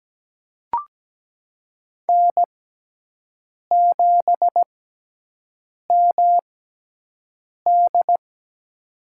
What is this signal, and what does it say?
Morse code, heard as a keyed tone, about 17 wpm, spaced as for 6 wpm, 700 Hz. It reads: N7MD
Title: USA Call Signs - 6x Character Spacing - 17wpm